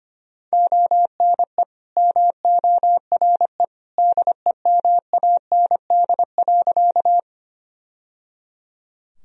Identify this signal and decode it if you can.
Morse code, heard as a keyed tone, about 25 wpm, 700 Hz. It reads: ONE MORE DEMAND.
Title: Sentences from Top 1000 Words - Review Part 4 - 25wpm